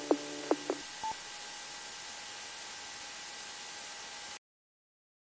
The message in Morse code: none

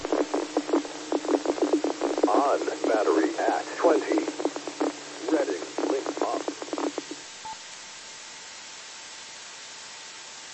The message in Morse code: none

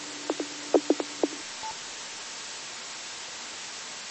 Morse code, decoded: none

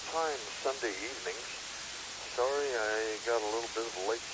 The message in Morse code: none

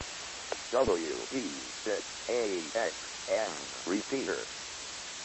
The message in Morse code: none

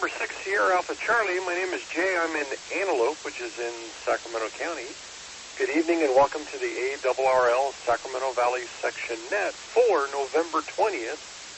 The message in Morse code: none

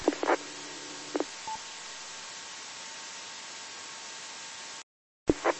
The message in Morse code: none

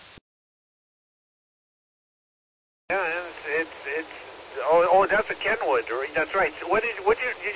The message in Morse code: none

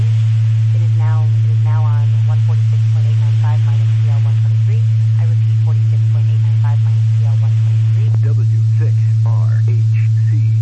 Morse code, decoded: none

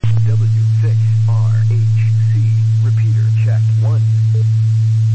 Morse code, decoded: none

